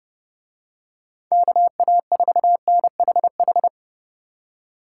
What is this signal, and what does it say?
Morse code, heard as a keyed tone, about 29 wpm, 700 Hz. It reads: KA4NHH